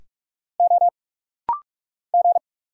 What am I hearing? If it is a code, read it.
Morse code, heard as a keyed tone, about 44 wpm, 700 Hz. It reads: O G
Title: Single Letter-Number Alt - Rapid-Fire 45wpm